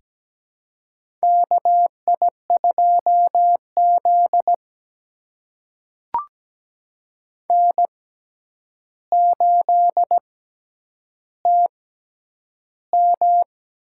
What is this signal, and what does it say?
Morse code, heard as a keyed tone, about 17 wpm, spaced as for 9 wpm, 700 Hz. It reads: KI2Z N8TM
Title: USA Call Signs - 4 Characters - 6x Character Spacing - 17wpm